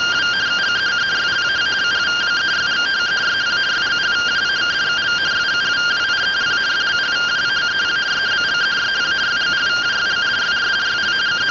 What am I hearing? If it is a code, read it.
Two-tone teleprinter signal, about 170 Hz shift.
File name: findings / axial iodine 3mm 205.gen-31 rtty rx